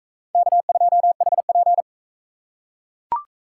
Morse code, heard as a keyed tone, about 39 wpm, 700 Hz. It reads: K2HP